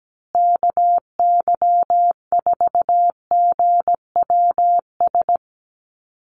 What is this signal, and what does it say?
Morse code, heard as a keyed tone, about 17 wpm, 700 Hz. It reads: KY4GWS
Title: USA Call Signs Repeated 3x 17wpm